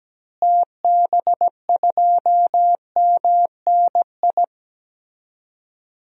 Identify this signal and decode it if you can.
Morse code, heard as a keyed tone, about 17 wpm, 700 Hz. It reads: TB2MNI
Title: DXCC-100 Call Signs - 6 Characters - Repeated 2x - 17wpm